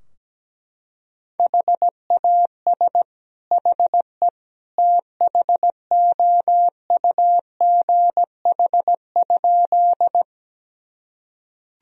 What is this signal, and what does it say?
Morse code, heard as a keyed tone, about 17 wpm, 700 Hz. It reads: HAS HE THOUGH?
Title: Sentences from Top 1000 Words - Review Part 2 - 17wpm